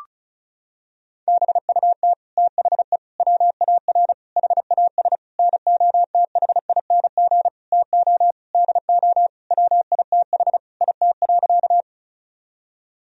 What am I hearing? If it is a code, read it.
Morse code, heard as a keyed tone, about 36 wpm, 700 Hz. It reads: BUT THE WAR HAS NOTHING TO DO WITH IT.
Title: Sentences from Top 1000 Words - Review Part 2 - 35wpm